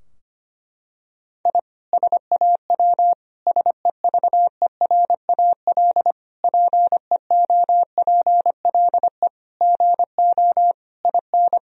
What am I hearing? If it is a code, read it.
Morse code, heard as a keyed tone, about 25 wpm, 700 Hz. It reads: I SAW SEVERAL PEOPLE GO IN
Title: Sentences from Top 1000 Words - Review Part 2 - 25wpm